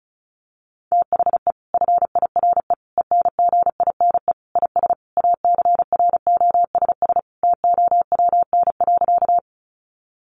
Morse code, heard as a keyed tone, about 34 wpm, 700 Hz. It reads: THE FIRE ENGINE IS ACROSS TOWN.
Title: Sentences from Top 1500 Words 35wpm